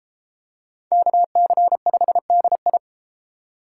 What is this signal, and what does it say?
Morse code, heard as a keyed tone, about 30 wpm, 700 Hz. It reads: KC5DI